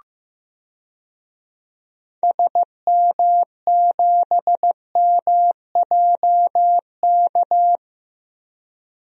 Morse code, heard as a keyed tone, about 15 wpm, 700 Hz. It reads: SM7MJK
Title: DXCC-100 Call Signs - 6 Characters - 15wpm